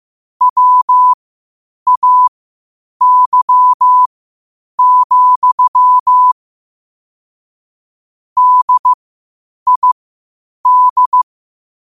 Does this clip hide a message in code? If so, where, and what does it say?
Yes: Morse code from 0.4 to 11.2 s, heard as a keyed tone, about 15 wpm, spaced as for 8 wpm, 1000 Hz. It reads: WAY, DID